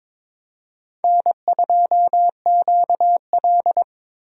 Morse code, heard as a keyed tone, about 22 wpm, 700 Hz. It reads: N2QL
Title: USA Call Signs - 4 Characters - 22wpm